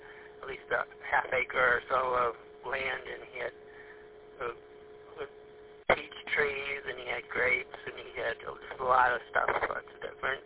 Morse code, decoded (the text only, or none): none